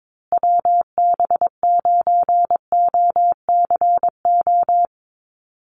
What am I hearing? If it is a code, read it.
Morse code, heard as a keyed tone, about 22 wpm, 700 Hz. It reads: WB9OCO
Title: USA Call Signs Repeated 3x 22wpm